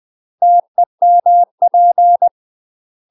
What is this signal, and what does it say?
Morse code, heard as a keyed tone, about 20 wpm, 700 Hz. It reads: TEMP
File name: UK MASTER QSOs TRAINING / 20WPM / UK MASTER QSOs 0022